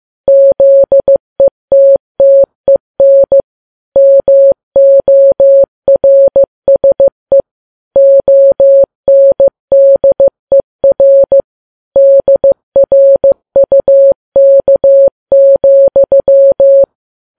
Morse code, heard as a keyed tone, about 15 wpm, 550 Hz. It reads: ZETTEN MORSE ONDER DRUK,
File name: Files Web tls / Morse on9fv / morse15kps